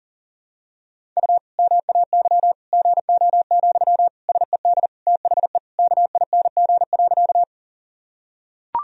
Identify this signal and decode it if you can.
Morse code, heard as a keyed tone, about 40 wpm, 700 Hz. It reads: U MAY GO, SED THE KING.
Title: Sentences from Top 400 Words and Abbreviations 40wpm